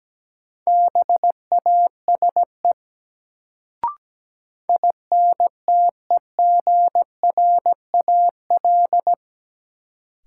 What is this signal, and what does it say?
Morse code, heard as a keyed tone, about 17 wpm, 700 Hz. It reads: BASE INTEGRAL
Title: Sets of 5 Words - New Word Drill - 17wpm